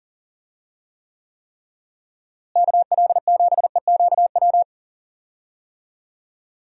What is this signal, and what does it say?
Morse code, heard as a keyed tone, about 40 wpm, 700 Hz. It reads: KL7EQW
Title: DXCC-100 Call Signs - 6 Characters - Repeated 3x - 40wpm